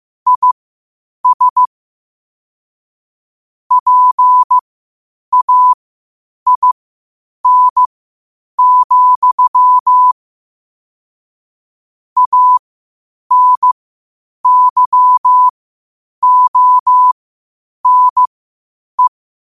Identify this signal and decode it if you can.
Morse code, heard as a keyed tone, about 15 wpm, spaced as for 8 wpm, 1000 Hz. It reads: IS PAIN, ANYONE
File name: qotdfiles / 8wpm